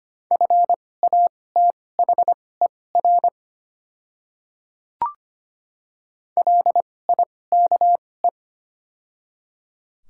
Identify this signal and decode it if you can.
Morse code, heard as a keyed tone, about 25 wpm, spaced as for 18 wpm, 700 Hz. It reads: FATHER LIKE
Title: 2x Character Spacing - Top 300 Words - 25wpm